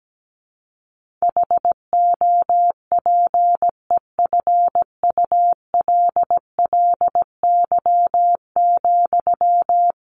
Morse code, heard as a keyed tone, about 17 wpm, 700 Hz. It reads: HOPEFULLY,
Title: Sentences from Top 1500 Words 17wpm